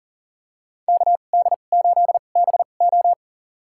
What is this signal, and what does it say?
Morse code, heard as a keyed tone, about 40 wpm, spaced as for 29 wpm, 700 Hz. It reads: KD8BO
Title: USA Call Signs - 5 Characters - 2x Character Spacing - 40wpm